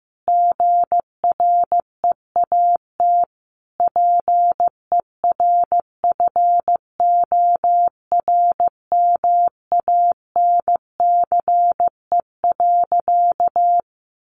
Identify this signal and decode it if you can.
Morse code, heard as a keyed tone, about 14 wpm, 700 Hz. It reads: GREAT PERFORMANCE.
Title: Sentences from Top 1000 Words 15wpm